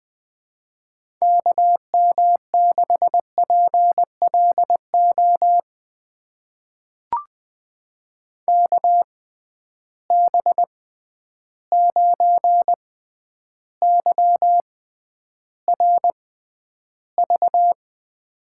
Morse code, heard as a keyed tone, about 20 wpm, spaced as for 10 wpm, 700 Hz. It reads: KM6PLO KB9YRV
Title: USA Call Signs - 6 Characters - 6x Character Spacing - 20wpm